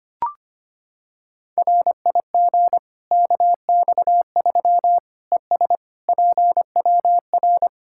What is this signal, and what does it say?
Morse code, heard as a keyed tone, about 25 wpm, 700 Hz. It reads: RIG KX3 ES PWR
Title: K1USN SST QSO Practice 25wpm